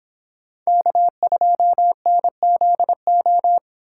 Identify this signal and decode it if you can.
Morse code, heard as a keyed tone, about 26 wpm, 700 Hz. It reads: K2NZO
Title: Speed Racing - USA Call Signs 22wpm